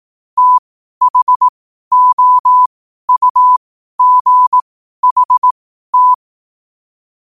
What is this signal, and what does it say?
Morse code, heard as a keyed tone, about 18 wpm, spaced as for 12 wpm, 1000 Hz. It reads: THOUGHT